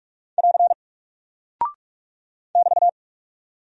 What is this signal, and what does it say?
Morse code, heard as a keyed tone, about 44 wpm, 700 Hz. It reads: <AR> <BT>
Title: Single Character (Letter:Number:Punctuation:Prosign) Alt 45wpm